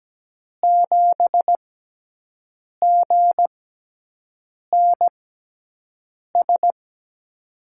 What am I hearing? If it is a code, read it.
Morse code, heard as a keyed tone, about 17 wpm, spaced as for 6 wpm, 700 Hz. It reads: 7GNS
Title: USA Call Signs - 6x Character Spacing - 17wpm